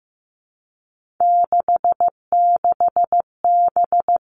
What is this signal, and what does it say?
Morse code, heard as a keyed tone, about 15 wpm, 700 Hz. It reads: 66B
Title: Three Challenging Letter-Number Combinations 15wpm